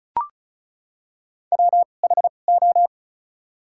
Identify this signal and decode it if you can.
Morse code, heard as a keyed tone, about 36 wpm, spaced as for 25 wpm, 700 Hz. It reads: WHO
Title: 2x Character Spacing - Top 100 Words - 35wpm